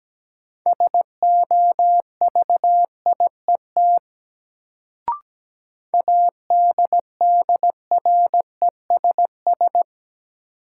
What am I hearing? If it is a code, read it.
Morse code, heard as a keyed tone, about 17 wpm, 700 Hz. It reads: SOVIET ADDRESS
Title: Sets of 5 Words Encore - New Word Drill - 17wpm